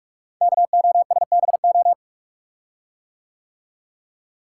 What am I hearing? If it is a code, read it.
Morse code, heard as a keyed tone, about 44 wpm, 700 Hz. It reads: KOSBO
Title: Code Groups - 5 Letters - 45wpm